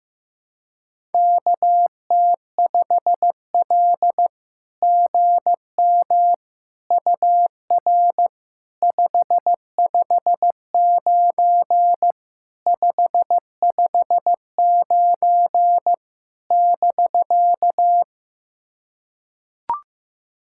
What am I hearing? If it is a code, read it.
Morse code, heard as a keyed tone, about 15 wpm, 700 Hz. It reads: KT5L GM UR 559 559 <BK>
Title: SOTA QSO Practice 15wpm